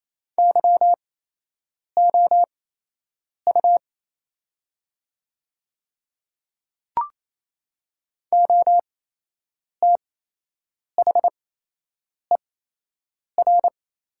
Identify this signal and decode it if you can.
Morse code, heard as a keyed tone, about 28 wpm, spaced as for 8 wpm, 700 Hz. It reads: YOU OTHER
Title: 8x Character Spacing - Top 100 Words - 28wpm